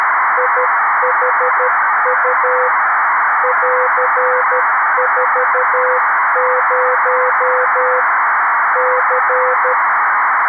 Morse code, heard as a keyed tone, about 13 wpm, 500 Hz, with noise. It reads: IHU <AR>40 C
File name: arissat / recordings / ARISSat 11Sept2011 1354utc 60frames